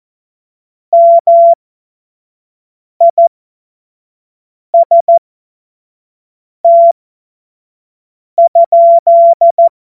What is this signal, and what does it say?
Morse code, heard as a keyed tone, about 14 wpm, spaced as for 5 wpm, 680 Hz. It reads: MIST?
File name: Morsekurs / Witze / Witz 17 Juni 2025 25 F70